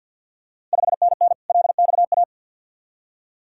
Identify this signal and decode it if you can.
Morse code, heard as a keyed tone, about 47 wpm, 690 Hz. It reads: 5NN LXA
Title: Ontario QSO Party Exchange 50wpm